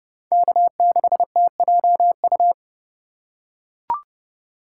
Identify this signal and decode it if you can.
Morse code, heard as a keyed tone, about 29 wpm, 700 Hz. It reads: K6TJU